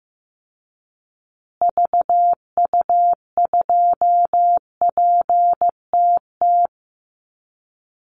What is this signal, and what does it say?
Morse code, heard as a keyed tone, about 15 wpm, 700 Hz. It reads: VU2PTT
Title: Call Signs Repeated 3x - 15wpm